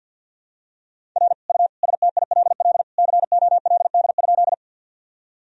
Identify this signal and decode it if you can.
Morse code, heard as a keyed tone, about 50 wpm, 700 Hz. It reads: R U STILL COLD?
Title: Sentences from Top 400 Words and Abbreviations 50wpm